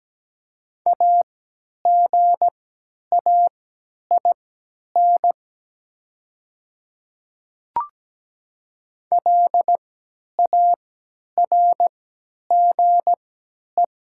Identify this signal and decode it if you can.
Morse code, heard as a keyed tone, about 17 wpm, spaced as for 10 wpm, 700 Hz. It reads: AGAIN LARGE
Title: 3x Character Spacing - Top 200 Words - 17wpm